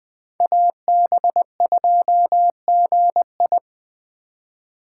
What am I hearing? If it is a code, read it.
Morse code, heard as a keyed tone, about 20 wpm, 700 Hz. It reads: AB2GI
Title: K1USN SST QSO Practice - 20wpm